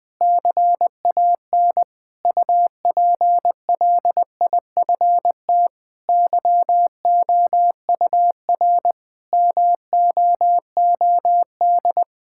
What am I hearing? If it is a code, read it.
Morse code, heard as a keyed tone, about 20 wpm, 700 Hz. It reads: CAN UPLIFT YOUR MOOD
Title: Difficult Sentences with Hints - 20wpm